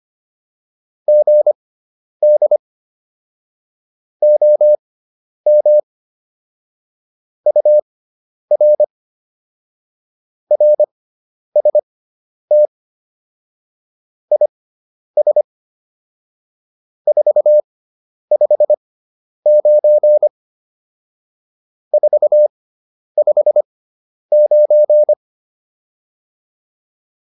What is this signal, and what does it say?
Morse code, heard as a keyed tone, about 25 wpm, spaced as for 10 wpm, 600 Hz. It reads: GD OM UR RST IS 459 459